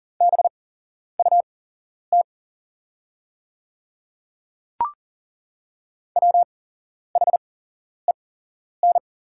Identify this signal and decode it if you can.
Morse code, heard as a keyed tone, about 39 wpm, spaced as for 11 wpm, 700 Hz. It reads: BUT WHEN
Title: 8x Character Spacing - Top 200 Words - 40wpm